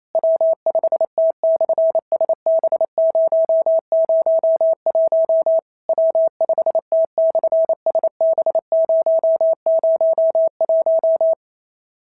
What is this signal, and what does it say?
Morse code, heard as a keyed tone, about 27 wpm, 650 Hz. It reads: W5T/SB001 W5T/SB001
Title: SOTA QSO Practice 28wpm